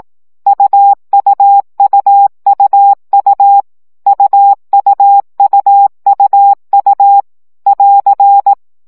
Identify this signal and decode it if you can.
Morse code, heard as a keyed tone, about 18 wpm, 800 Hz. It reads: UUUUU UUUUU <AR>
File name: CW gyakorlas u 15 baud